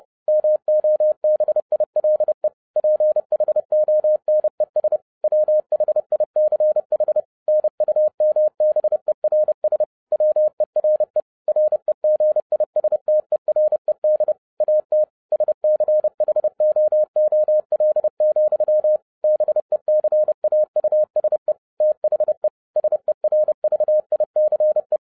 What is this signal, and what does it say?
Morse code, heard as a keyed tone, about 30 wpm, 610 Hz. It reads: MOBILE PHONES WHICH NUMBERS WERE REGISTERED AT SCHOOL, BECAUSE THE SERVICE